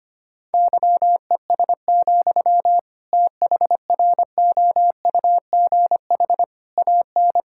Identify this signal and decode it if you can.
Morse code, heard as a keyed tone, about 25 wpm, 700 Hz. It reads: YES, THROUGH AN
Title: Sentences from Top 1600 Words 25wpm